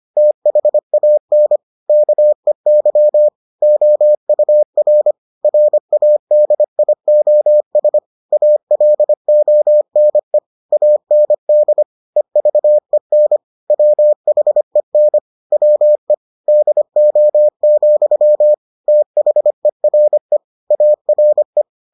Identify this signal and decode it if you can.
Morse code, heard as a keyed tone, about 25 wpm, 600 Hz. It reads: THAN KEY OUR RADIOS ALONE AND EVEN WHEN WE DO, THERE ARE